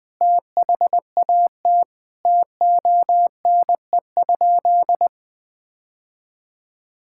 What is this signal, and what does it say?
Morse code, heard as a keyed tone, about 20 wpm, 700 Hz. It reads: THAT TONE?
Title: Sentences from Top 1600 Words 20wpm